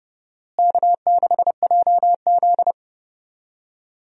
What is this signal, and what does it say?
Morse code, heard as a keyed tone, about 30 wpm, 700 Hz. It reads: K6JZ
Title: USA Call Signs - 4 Characters - Repeated 2x - 30wpm